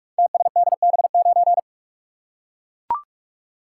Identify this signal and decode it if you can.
Morse code, heard as a keyed tone, about 44 wpm, 690 Hz. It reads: TSDB9
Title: Code Groups - 5 Letter:Numbers - 45wpm